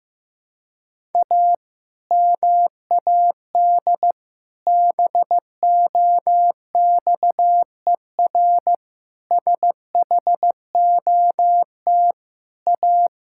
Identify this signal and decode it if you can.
Morse code, heard as a keyed tone, about 15 wpm, 700 Hz. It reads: A MAD BOXER SHOT A
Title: Pangram Challenge v2 - 15wpm